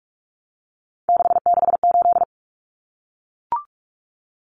Morse code, heard as a keyed tone, about 44 wpm, 700 Hz. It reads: 668
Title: Common Cut Numbers - 3-Digit 45wpm